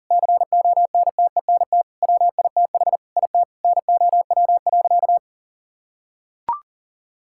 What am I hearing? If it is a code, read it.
Morse code, heard as a keyed tone, about 40 wpm, 700 Hz. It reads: CONTENT WITH IT NOW.
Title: Sentences from Top 1000 Words - Review Part 6 - 40wpm